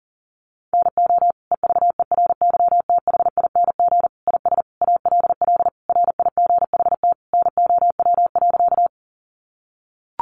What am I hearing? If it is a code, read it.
Morse code, heard as a keyed tone, about 40 wpm, 700 Hz. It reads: NO EVERYTHING IS ALL RIGHT NOW.
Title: Sentences from Top 700 Words 40wpm